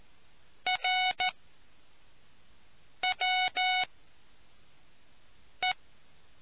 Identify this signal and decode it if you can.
Morse code, heard as a keyed tone, about 14 wpm, spaced as for 4 wpm, 740 Hz. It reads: RWE